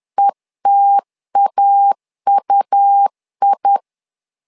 Morse code, heard as a keyed tone, about 10 wpm, 800 Hz. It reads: ETAUI
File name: LEZIONI / 2 UHDN / 2 U 10wpm 1622